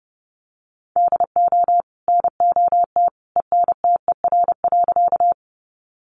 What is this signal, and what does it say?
Morse code, heard as a keyed tone, about 28 wpm, 700 Hz. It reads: DO NOT ENTER.
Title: Sentences from Top 1300 Words 30wpm